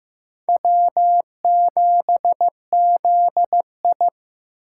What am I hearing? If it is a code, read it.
Morse code, heard as a keyed tone, about 15 wpm, 700 Hz. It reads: W7ZI
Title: USA Call Signs - 4 Characters - 3x Character Spacing - 15wpm